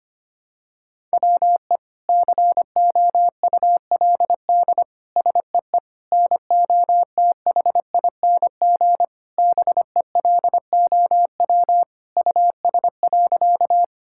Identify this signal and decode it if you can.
Morse code, heard as a keyed tone, about 25 wpm, 700 Hz. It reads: WE COULD SEE NOTHING BELOW US.
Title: Sentences from Top 1000 Words - Review Part 5 - 25wpm